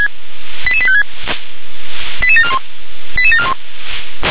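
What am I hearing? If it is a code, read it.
Five-tone selective call (ZVEI), heard as quick runs of five tones, tones of about 70 ms. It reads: 88856 88741 88741
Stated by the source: Five-tone ZVEI selective call received on an SDR.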